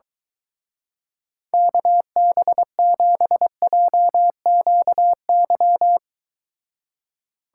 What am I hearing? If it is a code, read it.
Morse code, heard as a keyed tone, about 24 wpm, 700 Hz. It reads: KB7JQY